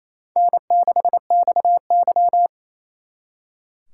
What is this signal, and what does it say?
Morse code, heard as a keyed tone, about 34 wpm, 700 Hz. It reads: N6XY
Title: Speed Racing - USA Call Signs 28wpm